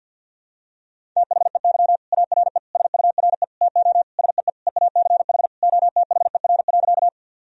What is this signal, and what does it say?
Morse code, heard as a keyed tone, about 52 wpm, 700 Hz. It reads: THEY ARE SURE TO SEE EACH OTHER.
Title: Sentences from Top 1000 Words - Review Part 1 - 50wpm